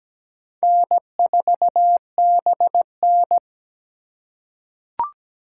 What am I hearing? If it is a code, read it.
Morse code, heard as a keyed tone, about 17 wpm, 700 Hz. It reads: N4BN
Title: USA Call Signs - 4 Characters - 4x Character Spacing - 17wpm